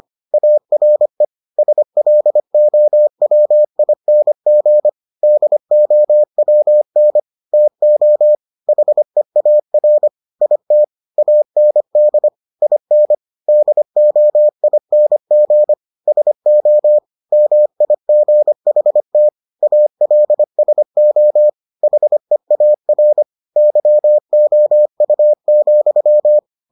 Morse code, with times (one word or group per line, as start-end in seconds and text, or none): 0.3-1.2 ARE
1.6-4.9 SLOWING
5.2-7.2 DOWN
7.5-8.3 TO
8.7-10.1 HEAR
10.4-10.8 IT
11.2-12.3 AND
12.6-13.1 IN
13.5-15.7 DOING
16.1-17.0 SO
17.3-19.3 MIGHT
19.6-21.5 ALSO
21.8-23.2 HEAR
23.6-26.4 YOU,